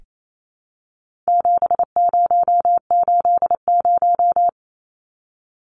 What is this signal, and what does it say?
Morse code, heard as a keyed tone, about 28 wpm, 700 Hz. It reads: 7080
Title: Numbers - 4-Digit 28wpm